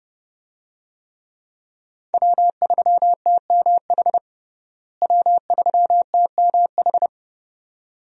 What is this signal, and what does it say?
Morse code, heard as a keyed tone, about 32 wpm, 700 Hz. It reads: W3TMH W3TMH
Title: USA Call Signs - 5 Characters - Repeated 2x - 30wpm